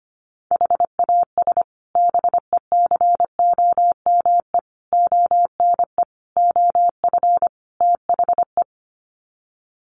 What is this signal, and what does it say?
Morse code, heard as a keyed tone, about 25 wpm, 700 Hz. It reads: HAS BECOME ONE OF THE
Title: Sets of 5 Words Encore 25wpm